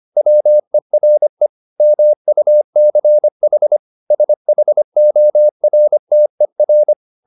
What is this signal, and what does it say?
Morse code, heard as a keyed tone, about 25 wpm, 600 Hz. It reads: WERE MUCH SHORTER